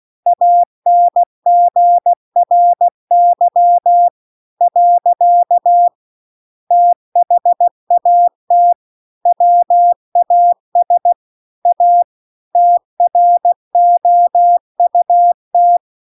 Morse code, heard as a keyed tone, about 16 wpm, 700 Hz. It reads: ANGRY . THAT WAS A TROUT
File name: CW-The-Big-Twoheaded-River-2-16WPM